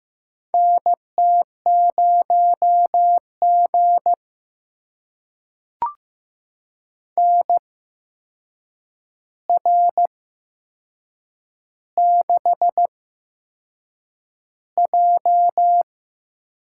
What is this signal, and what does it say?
Morse code, heard as a keyed tone, about 15 wpm, spaced as for 6 wpm, 700 Hz. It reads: NT0G NR6J
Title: USA Call Signs - 4 Characters - 8x Character Spacing - 15wpm